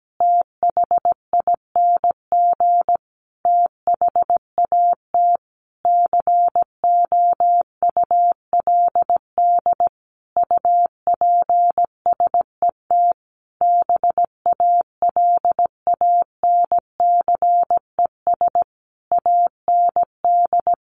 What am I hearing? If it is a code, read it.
Morse code, heard as a keyed tone, about 17 wpm, 700 Hz. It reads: THING THAT COULD UPSET BALANCES AND
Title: A Sound Of Thunder 17wpm